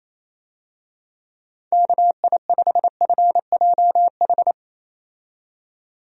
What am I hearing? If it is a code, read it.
Morse code, heard as a keyed tone, about 28 wpm, 700 Hz. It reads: KI5FJH
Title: USA Call Signs - 6 Characters - 28wpm